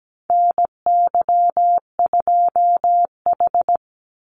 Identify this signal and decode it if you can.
Morse code, heard as a keyed tone, about 17 wpm, 700 Hz. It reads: NY2H
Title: Call Signs Repeated 2x - 17wpm